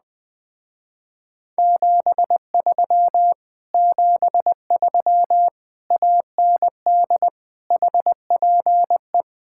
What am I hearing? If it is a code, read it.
Morse code, heard as a keyed tone, about 20 wpm, 700 Hz. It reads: 73 73 AND HPE
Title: Speed-Racing Common QSO Phrases - 17wpm